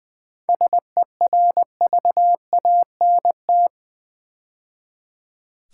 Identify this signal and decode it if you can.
Morse code, heard as a keyed tone, about 20 wpm, 700 Hz. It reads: SERVANT
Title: 7 Letter Words - First 3 Letters Given - 20wpm